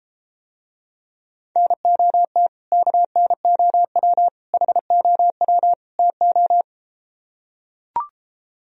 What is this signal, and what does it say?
Morse code, heard as a keyed tone, about 33 wpm, 700 Hz. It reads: NOT KNOW HOW TO